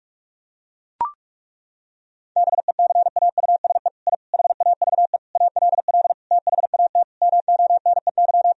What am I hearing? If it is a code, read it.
Morse code, heard as a keyed tone, about 41 wpm, 700 Hz. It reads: BECAUSE I HAVE ALL THAT MONEY
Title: Sentences from Top 1000 Words - Review Part 4 - 45wpm